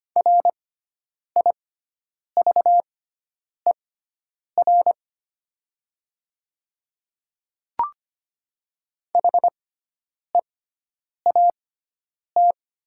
Morse code, heard as a keyed tone, about 25 wpm, spaced as for 9 wpm, 700 Hz. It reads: RIVER HEAT
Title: 6x Character Spacing - Top 500 Words - 25wpm